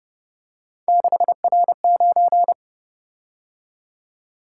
Morse code, heard as a keyed tone, about 29 wpm, 700 Hz. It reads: 6R9